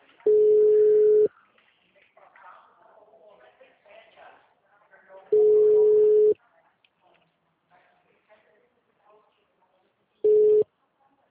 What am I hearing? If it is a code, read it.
Ringback tone (Europe, 425 Hz), with speech.
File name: out- -2031- - - .71848